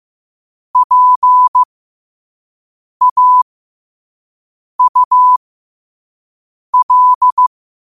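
Morse code, heard as a keyed tone, about 15 wpm, spaced as for 5 wpm, 1000 Hz. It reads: PAUL
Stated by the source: Morse code at 5 words per minute.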